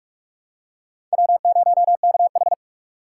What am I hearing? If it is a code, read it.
Morse code, heard as a keyed tone, about 46 wpm, 700 Hz. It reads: W0KH
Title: USA Call Signs - 6x Character Spacing - 45wpm